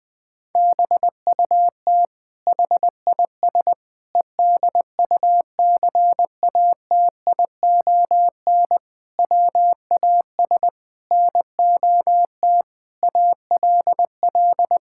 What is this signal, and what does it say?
Morse code, heard as a keyed tone, about 20 wpm, 700 Hz. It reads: BUT HIS EDUCATION WAS NOT ALL
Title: Sentences from Top 1000 Words - Review Part 6 - 20wpm